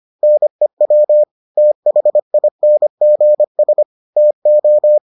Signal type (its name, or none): Morse code